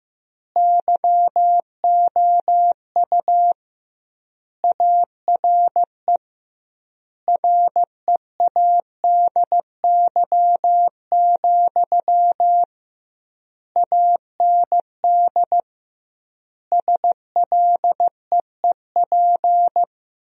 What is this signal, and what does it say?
Morse code, heard as a keyed tone, about 15 wpm, 700 Hz. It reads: YOU ARE READY, AND SLEEP
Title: 2x Word-Spacing - Sentences from Top 500 Words - 15wpm